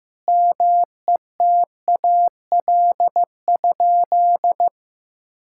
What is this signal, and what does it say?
Morse code, heard as a keyed tone, about 15 wpm, 700 Hz. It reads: METAL?
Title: Sentences from Top 1600 Words 15wpm